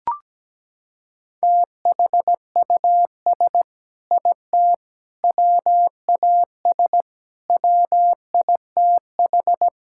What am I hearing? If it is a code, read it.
Morse code, heard as a keyed tone, about 17 wpm, 700 Hz. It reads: THUS IT WAS WITH
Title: Sentences from Top 1000 Words - Review Part 5 - 17wpm